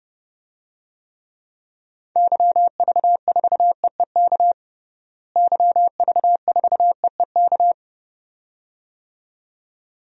Morse code, heard as a keyed tone, about 30 wpm, 700 Hz. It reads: YV4EEK YV4EEK
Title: DXCC-100 Call Signs - 6 Characters - Repeated 2x - 30wpm